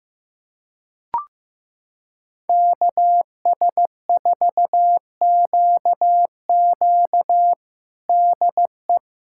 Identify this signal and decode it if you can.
Morse code, heard as a keyed tone, about 15 wpm, 700 Hz. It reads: KS4QQ DE
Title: USA Callsign DE Callsign - 5 Characters - 15wpm